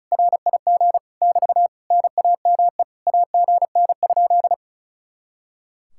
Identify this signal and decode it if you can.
Morse code, heard as a keyed tone, about 45 wpm, 700 Hz. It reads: RIG <BT> NAME AGN?